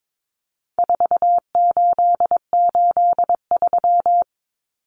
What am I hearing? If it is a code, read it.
Morse code, heard as a keyed tone, about 22 wpm, 700 Hz. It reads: 4883
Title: Numbers - 4-Digit 22wpm